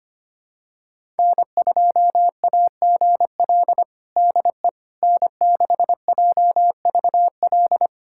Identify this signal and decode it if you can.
Morse code, heard as a keyed tone, about 25 wpm, 700 Hz. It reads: N2AGL DE N6JVL
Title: USA Callsign DE Callsign - 5 Characters - 25wpm